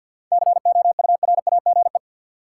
Morse code, heard as a keyed tone, about 49 wpm, 700 Hz. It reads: COURAGE
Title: Seven Letter Words - 50wpm